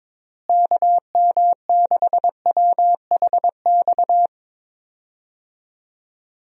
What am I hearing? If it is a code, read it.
Morse code, heard as a keyed tone, about 22 wpm, 700 Hz. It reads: KM6WHX
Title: USA Call Signs - 6 Characters - Repeated 3x - 22wpm